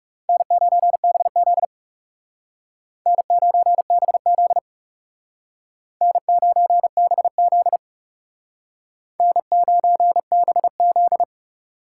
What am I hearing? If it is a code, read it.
Morse code, heard as a keyed tone, about 39 wpm, 700 Hz. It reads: N9BZ N9BZ N9BZ N9BZ